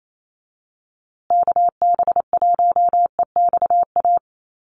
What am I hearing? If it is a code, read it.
Morse code, heard as a keyed tone, about 28 wpm, 700 Hz. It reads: KB1EXA
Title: USA Call Signs Repeated 3x 28wpm